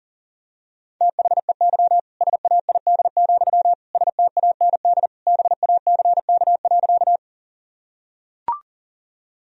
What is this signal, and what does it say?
Morse code, heard as a keyed tone, about 37 wpm, 700 Hz. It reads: THEY SAID, STAND BACK.
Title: Sentences from Top 1000 Words - Review Part 1 - 40wpm